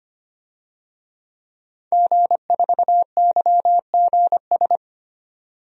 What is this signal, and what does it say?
Morse code, heard as a keyed tone, about 25 wpm, 700 Hz. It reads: G4YGS
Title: DXCC-100 Call Signs - 5 Characters - 25wpm